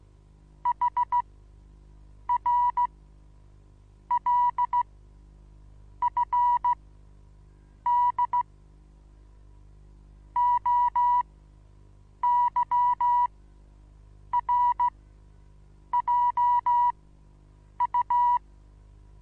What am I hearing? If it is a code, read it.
Morse code, heard as a keyed tone, about 16 wpm, spaced as for 7 wpm, 970 Hz. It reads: HRLFD OYRJU
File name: Onlinelehrgang / morsen / DJ4UF Morsen25